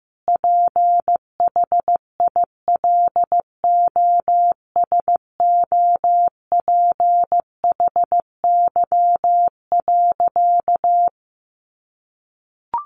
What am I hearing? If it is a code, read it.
Morse code, heard as a keyed tone, about 14 wpm, 700 Hz. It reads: PHILOSOPHY.
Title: Sentences from Top 900 Words 15wpm